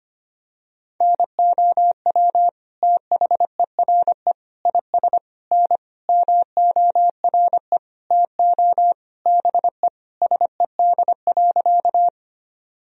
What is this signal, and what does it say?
Morse code, heard as a keyed tone, about 25 wpm, 700 Hz. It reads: NOW THERE IS N MORE TO BE SED.
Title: Sentences from Top 100 Words and Abbreviations 25wpm